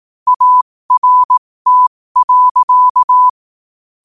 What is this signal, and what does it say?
Morse code, heard as a keyed tone, about 18 wpm, spaced as for 15 wpm, 1000 Hz. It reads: ART.
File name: qotdfiles / 15wpm